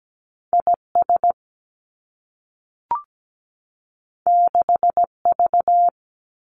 Morse code, heard as a keyed tone, about 17 wpm, 700 Hz. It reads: IS 6V
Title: Two Challenging Letter-Number Combinations 17wpm